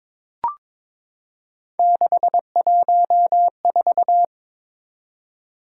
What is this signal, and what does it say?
Morse code, heard as a keyed tone, about 22 wpm, 700 Hz. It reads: 614
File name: practice / 22wpm / Three Letter-Number Combinations - 22wpm